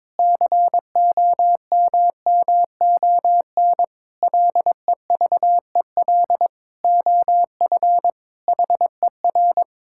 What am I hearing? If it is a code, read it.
Morse code, heard as a keyed tone, about 22 wpm, 700 Hz. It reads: COMMON LEVEL OF HER
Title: Sentences from Top 1600 Words 22wpm